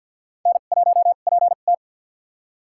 Morse code, heard as a keyed tone, about 50 wpm, spaced as for 36 wpm, 700 Hz. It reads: N1PT
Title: USA Call Signs - 4 Characters - 2x Character Spacing - 50wpm